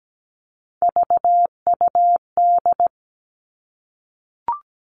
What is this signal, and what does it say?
Morse code, heard as a keyed tone, about 17 wpm, 700 Hz. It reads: VUD
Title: All Cut Numbers - 3-Digit 17wpm